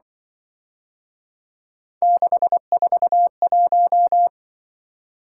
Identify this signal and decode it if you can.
Morse code, heard as a keyed tone, about 23 wpm, 700 Hz. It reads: 641
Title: Speed-Racing - Three Letter-Number Combinations - 20wpm